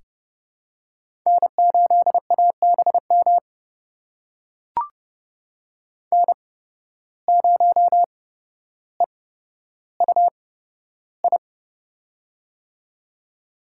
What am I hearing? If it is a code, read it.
Morse code, heard as a keyed tone, about 30 wpm, spaced as for 13 wpm, 700 Hz. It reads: N8ABM N0EUI
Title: USA Call Signs - 5 Characters - 8x Character Spacing - 30wpm